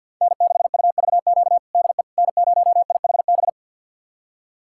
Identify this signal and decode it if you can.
Morse code, heard as a keyed tone, about 51 wpm, 690 Hz. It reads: N6UVQ DE N0IHB
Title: USA Callsign DE Callsign - 5 Characters - 50wpm